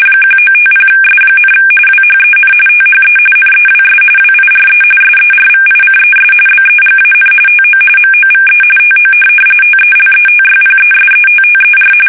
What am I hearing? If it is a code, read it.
RTTY, about 45.45 baud. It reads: 1950S, WHEN THE NEWS WIRE SERVICES BEGAN TO MOVE TO TELETYPESETTER
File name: fsk850 / teletype-corp